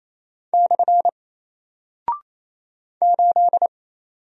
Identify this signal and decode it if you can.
Morse code, heard as a keyed tone, about 27 wpm, 700 Hz. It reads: / 8
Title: Single Character (Letter:Number:Punctuation:Prosign) 28wpm